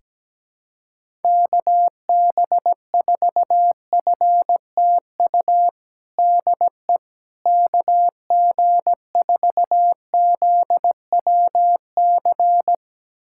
Morse code, heard as a keyed tone, about 17 wpm, 700 Hz. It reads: KB4FTU DE KG4ZWC
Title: USA Callsign DE Callsign - 6 Characters - 17wpm